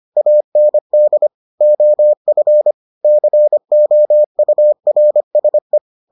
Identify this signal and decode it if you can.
Morse code, heard as a keyed tone, about 25 wpm, 600 Hz. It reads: AND OF COURSE